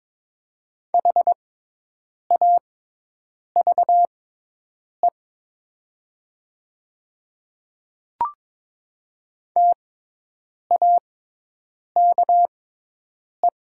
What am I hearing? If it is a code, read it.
Morse code, heard as a keyed tone, about 22 wpm, spaced as for 8 wpm, 700 Hz. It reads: HAVE TAKE
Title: 6x Character Spacing - Top 300 Words - 22wpm